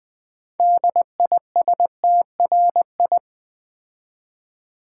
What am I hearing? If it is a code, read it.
Morse code, heard as a keyed tone, about 20 wpm, 700 Hz. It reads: DISTRI
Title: Build A Word - 6 Letters At A Time - 12-Letter Words - 20wpm